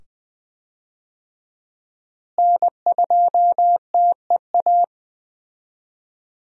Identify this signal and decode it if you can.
Morse code, heard as a keyed tone, about 20 wpm, 700 Hz. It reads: N2TEA
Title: USA Call Signs - 5 Characters - Repeated 2x - 20wpm